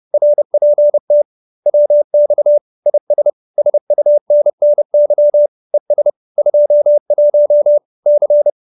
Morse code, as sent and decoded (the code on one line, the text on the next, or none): .-. .--. - / .-- -..- / .. ... / ... ..- -. -. -.-- / . ... / ..--- .---- / -.-.
RPT WX IS SUNNY ES 21 C